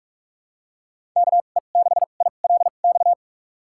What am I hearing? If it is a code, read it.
Morse code, heard as a keyed tone, about 46 wpm, spaced as for 33 wpm, 690 Hz. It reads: KE6ILX
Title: USA Call Signs - 6 Characters - 2x Character Spacing - 45wpm